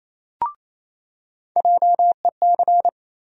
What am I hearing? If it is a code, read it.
Morse code, heard as a keyed tone, about 28 wpm, 700 Hz. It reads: JEC
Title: USA Call Sign Postfixes - 3 Letters 28wpm